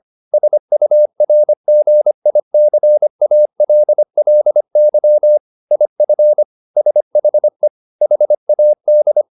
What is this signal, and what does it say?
Morse code, heard as a keyed tone, about 25 wpm, 600 Hz. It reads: SURGICALLY IF SHE HAD